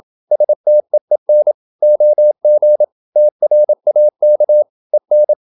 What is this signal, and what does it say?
Morse code, heard as a keyed tone, about 27 wpm, 600 Hz. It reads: STEEN OG TRAK EN
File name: Ferie QRQ 27wpm